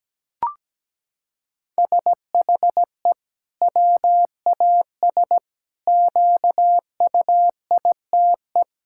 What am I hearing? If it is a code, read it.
Morse code, heard as a keyed tone, about 17 wpm, 700 Hz. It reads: SHE WAS QUITE
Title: Sentences from Top 1000 Words - Review Part 5 - 17wpm